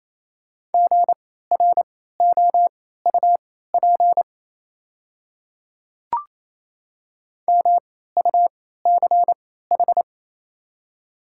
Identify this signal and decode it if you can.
Morse code, heard as a keyed tone, about 27 wpm, spaced as for 16 wpm, 690 Hz. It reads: GROUP MUCH
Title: 3x Character Spacing - Top 300 Words - 28wpm